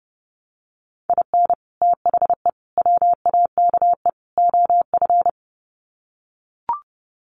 Morse code, heard as a keyed tone, about 30 wpm, 700 Hz. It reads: IN THE WAKE OF
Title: Sets of 4 Words 30wpm